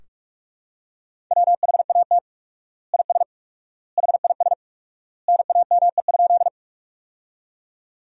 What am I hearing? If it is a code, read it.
Morse code, heard as a keyed tone, about 41 wpm, 700 Hz. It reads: WHAT IS HIS NAME?
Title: 4x Word-Spacing - Sentences from Top 200 Words - 45wpm